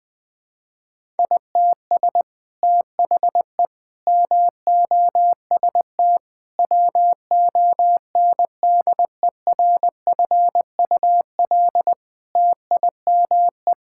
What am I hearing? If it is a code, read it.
Morse code, heard as a keyed tone, about 21 wpm, 700 Hz. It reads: ITS THE MOST WONDERFUL TIME
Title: Sets of 8 Words - 20wpm